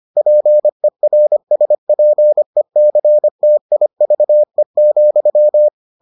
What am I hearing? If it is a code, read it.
Morse code, heard as a keyed tone, about 25 wpm, 600 Hz. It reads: PERSPECTIVE,